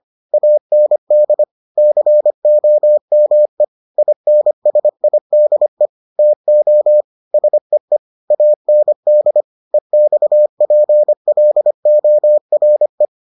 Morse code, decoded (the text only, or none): AND COME INSIDE TO SEE AND EXPLORE